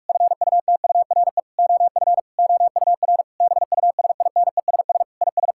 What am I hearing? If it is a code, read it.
Morse code, heard as a keyed tone, about 46 wpm, 700 Hz. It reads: FUTURE OF OUR BUSINESS IS
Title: Sentences from Top 1000 Words - Review Part 6 - 45wpm